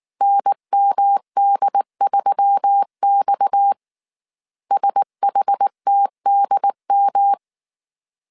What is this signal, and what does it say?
Morse code, heard as a keyed tone, about 19 wpm, 800 Hz. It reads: NMD3X SHTDM
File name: CW / DET / 1 6 1500C S A 18wpm 2255